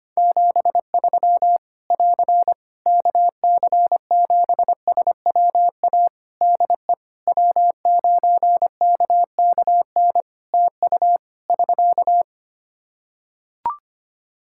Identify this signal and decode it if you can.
Morse code, heard as a keyed tone, about 25 wpm, 700 Hz. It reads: 73 <AR> KC7SWA DE W9KKN TU <SK>
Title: Practice QSO - Exchange 4 - Ending - 25wpm